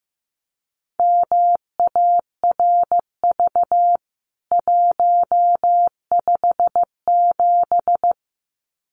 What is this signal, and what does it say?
Morse code, heard as a keyed tone, about 15 wpm, 700 Hz. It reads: MARV 157
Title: CWT Exchange 15wpm